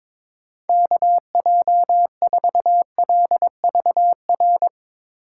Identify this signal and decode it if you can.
Morse code, heard as a keyed tone, about 21 wpm, 700 Hz. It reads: KJ4LVR
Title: USA Call Signs - 6 Characters - 4x Character Spacing - 22wpm